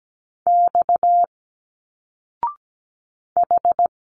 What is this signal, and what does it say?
Morse code, heard as a keyed tone, about 17 wpm, 700 Hz. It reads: X H
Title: Single Letters 17wpm